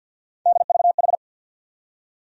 Morse code, heard as a keyed tone, about 51 wpm, 700 Hz. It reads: DVH
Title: USA Call Sign Postfixes - 3 Letters 50wpm